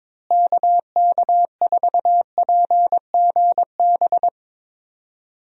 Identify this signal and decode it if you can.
Morse code, heard as a keyed tone, about 22 wpm, 700 Hz. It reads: KK4PGB
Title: USA Call Signs - 6 Characters - Repeated 2x - 22wpm